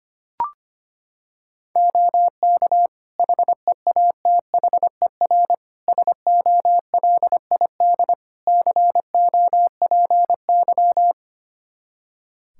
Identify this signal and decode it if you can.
Morse code, heard as a keyed tone, about 25 wpm, 700 Hz. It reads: OK HEATHER SOLID COPY
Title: K1USN SST QSO Practice 25wpm